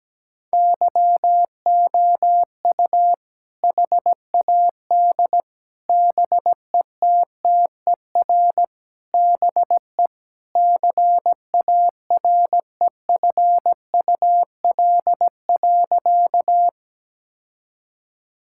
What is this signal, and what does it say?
Morse code, heard as a keyed tone, about 17 wpm, 700 Hz. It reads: YOU HAD BETTER BE CAREFUL.
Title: Sentences from Top 1600 Words 17wpm